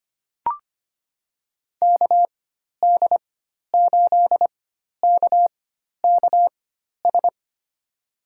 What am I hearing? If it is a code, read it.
Morse code, heard as a keyed tone, about 25 wpm, spaced as for 12 wpm, 700 Hz. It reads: KD8KKS
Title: USA Call Signs - 6 Characters - 4x Character Spacing - 25wpm